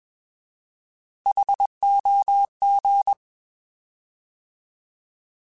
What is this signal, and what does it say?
Morse code, heard as a keyed tone, about 21 wpm, 800 Hz. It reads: HOG
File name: Alphabetical / Alphabetical mp3s / Hs @20